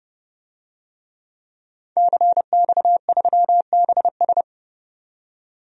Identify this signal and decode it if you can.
Morse code, heard as a keyed tone, about 30 wpm, 700 Hz. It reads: CX3BS
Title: DXCC-100 Call Signs - 5 Characters - 30wpm